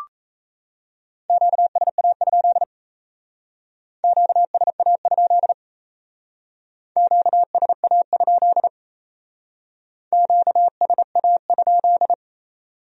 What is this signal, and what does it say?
Morse code, heard as a keyed tone, about 35 wpm, 700 Hz. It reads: QSA? QSA? QSA? QSA?
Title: Speed-Racing Q-Codes - 28wpm